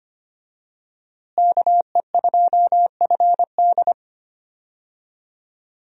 Morse code, heard as a keyed tone, about 25 wpm, 700 Hz. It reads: KE2FD
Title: USA Call Signs - 5 Characters - Repeated 2x - 25wpm